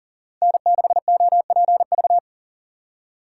Morse code, heard as a keyed tone, about 49 wpm, 700 Hz. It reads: N6OPV
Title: Speed Racing - USA 1x3 Call Signs 40wpm